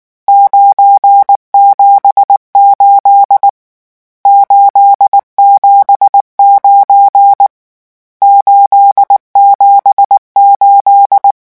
Morse code, heard as a keyed tone, about 19 wpm, 800 Hz. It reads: 978 879 878
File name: morse mp3 / lecon12-vitesse 18